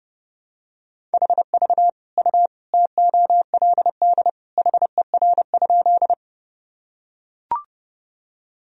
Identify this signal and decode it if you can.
Morse code, heard as a keyed tone, about 30 wpm, 700 Hz. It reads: HV U TOLD HER?
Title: Sentences from Top 400 Words and Abbreviations 30wpm